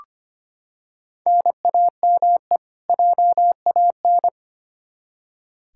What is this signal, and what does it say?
Morse code, heard as a keyed tone, about 25 wpm, 700 Hz. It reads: NAME JAN
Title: Practice QSO - Exchange 1 - Beginning - 25wpm